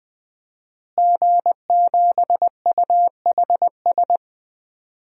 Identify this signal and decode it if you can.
Morse code, heard as a keyed tone, about 20 wpm, 700 Hz. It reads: G7UHS
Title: DXCC-100 Call Signs - 5 Characters - Repeated 3x - 20wpm